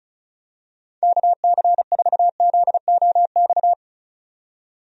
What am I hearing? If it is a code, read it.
Morse code, heard as a keyed tone, about 36 wpm, 700 Hz. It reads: KC4ZOX
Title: USA Call Signs - 6 Characters - 35wpm